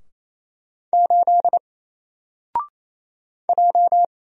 Morse code, heard as a keyed tone, about 28 wpm, 700 Hz. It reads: 8 J
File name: Single-Letter-Number-28wpm